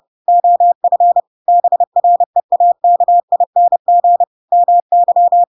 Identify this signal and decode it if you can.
Morse code, heard as a keyed tone, about 30 wpm, 700 Hz. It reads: OF BREAKING MY
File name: Pinocchio / Pinocchio ch5 30 30 ik1hge